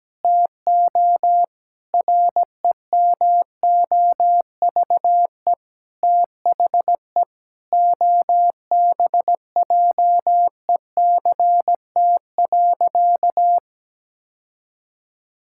Morse code, heard as a keyed tone, about 17 wpm, 700 Hz. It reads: TO REMOVE THE OBJECT.
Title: Sentences from Top 1000 Words - Review Part 5 - 17wpm